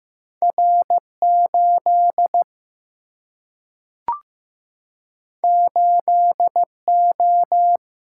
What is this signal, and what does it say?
Morse code, heard as a keyed tone, about 15 wpm, 700 Hz. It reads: R 8 8 O
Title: Single-Letter Single-Number 15wpm